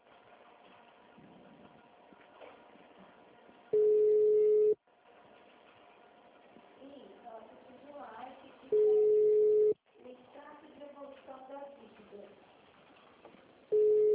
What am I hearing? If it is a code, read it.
Ringback tone (Europe, 425 Hz), with speech.